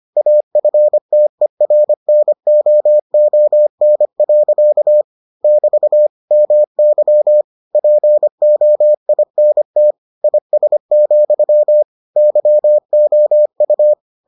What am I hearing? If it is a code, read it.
Morse code, heard as a keyed tone, about 25 wpm, 600 Hz. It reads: AFTERNOON. <BT> MY POINT IS, YOU